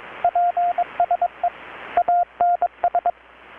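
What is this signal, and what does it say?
Morse code, heard as a keyed tone, about 22 wpm, 720 Hz, with radio noise. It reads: PSE ANS
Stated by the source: Morse from KPH's Night of Nights XX maritime radiotelegraphy event, received in Chiapas, Mexico about 3,800 km away on a Kenwood R-600 with a grounded 10 m long-wire antenna.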